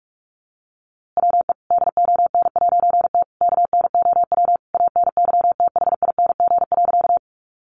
Morse code, heard as a keyed tone, about 44 wpm, 700 Hz. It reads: WE DON'T KNOW ANYTHING.
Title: A Sound Of Thunder 45wpm